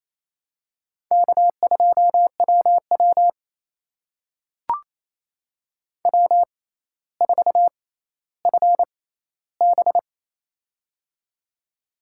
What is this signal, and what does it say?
Morse code, heard as a keyed tone, about 28 wpm, spaced as for 14 wpm, 700 Hz. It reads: K2WW W4FB
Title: USA Call Signs - 4 Characters - 6x Character Spacing - 28wpm